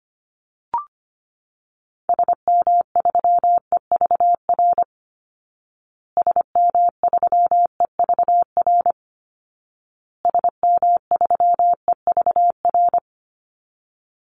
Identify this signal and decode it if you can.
Morse code, heard as a keyed tone, about 25 wpm, 700 Hz. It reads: SM3EVR SM3EVR SM3EVR
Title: Call Signs Repeated 2x - 25wpm